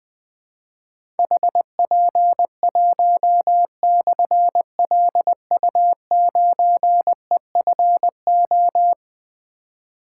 Morse code, heard as a keyed tone, about 20 wpm, 700 Hz. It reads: HP1/LU9EFO
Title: Difficult Call Signs 20wpm